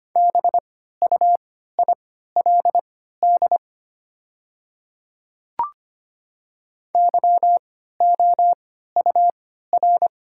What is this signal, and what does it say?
Morse code, heard as a keyed tone, about 25 wpm, spaced as for 14 wpm, 690 Hz. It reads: BUILD YOUR
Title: 3x Character Spacing - Top 200 Words - 25wpm